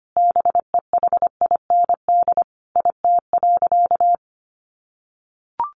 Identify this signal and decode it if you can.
Morse code, heard as a keyed tone, about 25 wpm, 700 Hz. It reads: BEHIND IT.
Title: Sentences from Top 500 Words 25wpm